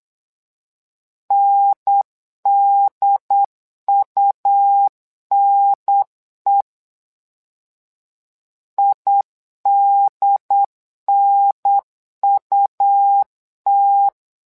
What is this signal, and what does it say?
Morse code, heard as a keyed tone, about 8 wpm, 800 Hz. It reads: NDUNE IDNUT
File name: LEZIONI / 2 UHDN / 2 UHDN 8wpm2232